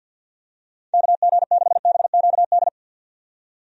Morse code, heard as a keyed tone, about 52 wpm, 700 Hz. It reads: KG6BQD
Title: USA Call Signs - 6 Characters - 50wpm